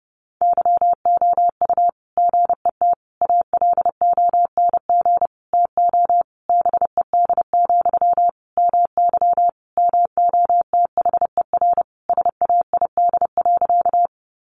Morse code, heard as a keyed tone, about 30 wpm, 700 Hz. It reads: YOU GET ALONG TO BED, MY MOTHER SAID.
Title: Sentences from Top 700 Words 30wpm